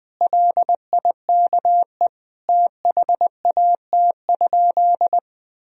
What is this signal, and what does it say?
Morse code, heard as a keyed tone, about 20 wpm, 700 Hz. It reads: LIKE THAT?
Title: Sentences from Top 1000 Words - Review Part 4 - 20wpm